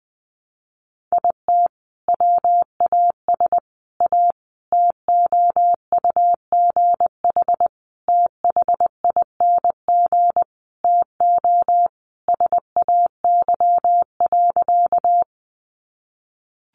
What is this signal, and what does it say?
Morse code, heard as a keyed tone, about 20 wpm, 700 Hz. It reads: IT WAS A TOUGH THING TO SAY.
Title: Sentences from Top 1200 Words 20wpm